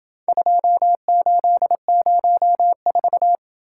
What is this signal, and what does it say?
Morse code, heard as a keyed tone, about 28 wpm, 700 Hz. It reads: 2804